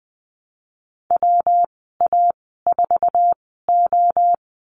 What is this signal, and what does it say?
Morse code, heard as a keyed tone, about 20 wpm, spaced as for 15 wpm, 700 Hz. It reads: WA4O
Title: 193 Lesson 31g - Letter v - Call Signs 20wpm